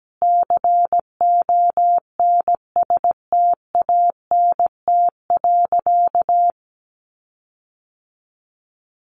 Morse code, heard as a keyed tone, about 17 wpm, 700 Hz. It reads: CONSTANT.
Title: Sentences from Top 1500 Words 17wpm